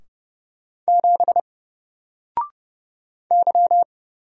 Morse code, heard as a keyed tone, about 30 wpm, 700 Hz. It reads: 7 Y
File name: Single-Letter-Number-30wpm